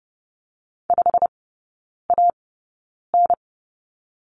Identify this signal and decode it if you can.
Morse code, heard as a keyed tone, about 30 wpm, spaced as for 9 wpm, 700 Hz. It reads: 5AN
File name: Lesson 17e - Letter u - Three Characters 30wpm